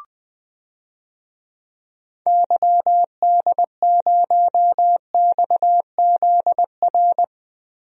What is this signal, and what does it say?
Morse code, heard as a keyed tone, about 20 wpm, 700 Hz. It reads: YD0XZR
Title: DXCC-100 Call Signs - 6 Characters - Repeated 2x - 20wpm